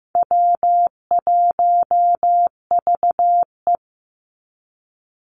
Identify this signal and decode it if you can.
Morse code, heard as a keyed tone, about 15 wpm, 700 Hz. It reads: W1VE
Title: Easy Call Signs Repeated 2x - 15wpm